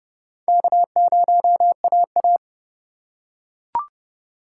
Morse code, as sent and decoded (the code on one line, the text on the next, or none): -.- ----- .- .-
K0AA